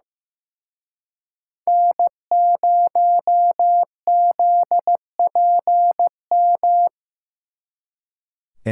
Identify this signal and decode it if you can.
Morse code, heard as a keyed tone, about 18 wpm, 700 Hz. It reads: N0ZPM
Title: Speed Racing - USA 1x3 Call Signs 15wpm